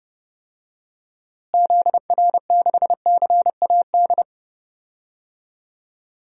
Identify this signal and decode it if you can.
Morse code, heard as a keyed tone, about 31 wpm, 700 Hz. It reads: ZR6CAD
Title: DXCC-100 Call Signs - 6 Characters - Repeated 2x - 30wpm